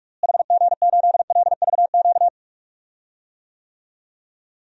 Morse code, heard as a keyed tone, about 46 wpm, 700 Hz. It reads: HG8LVQ
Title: DXCC-100 Call Signs - 6 Characters - Repeated 3x - 45wpm